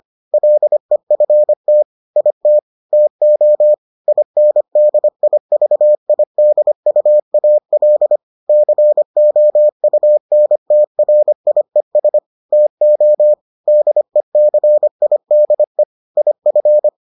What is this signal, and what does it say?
Morse code, heard as a keyed tone, about 25 wpm, 600 Hz. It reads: LEFT IT TO INDIVIDUAL COUNTRIES TO DECIDE IF